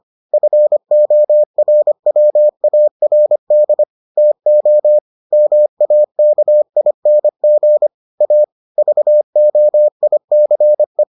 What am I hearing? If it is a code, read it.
Morse code, heard as a keyed tone, about 25 wpm, 600 Hz. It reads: FORWARD TO MAKING A VOICE